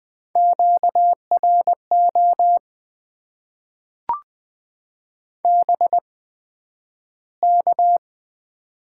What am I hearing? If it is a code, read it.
Morse code, heard as a keyed tone, about 20 wpm, spaced as for 11 wpm, 700 Hz. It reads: QRO BK
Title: Top 90 QSO Elements - 8x Character Spacing - 20wpm